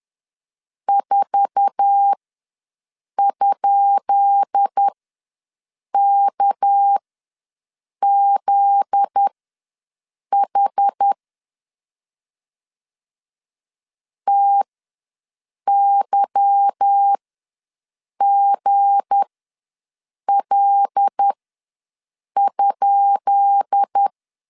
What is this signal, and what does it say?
Morse code, heard as a keyed tone, about 10 wpm, spaced as for 6 wpm, 800 Hz. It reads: 4?KZH TYGL?
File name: LEZIONI / 10 SIGNS / 10B INT ws 10wpm 4134